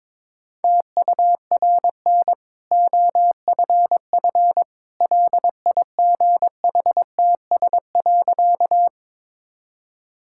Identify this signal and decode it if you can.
Morse code, heard as a keyed tone, about 22 wpm, 690 Hz. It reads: TURN OFF LIGHTS.
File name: practice / 22wpm / Easy Sentences with Hints - 22wpm